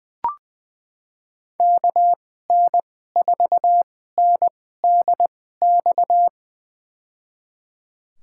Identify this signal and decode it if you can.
Morse code, heard as a keyed tone, about 20 wpm, spaced as for 14 wpm, 700 Hz. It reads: KN4NDX
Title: USA Call Signs - 6 Characters - 2x Character Spacing - 20wpm